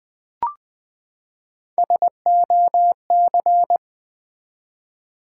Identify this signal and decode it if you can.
Morse code, heard as a keyed tone, about 20 wpm, 700 Hz. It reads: SOC
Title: Build A Word - 3 Letters At A Time - 6-Letter Words - 20wpm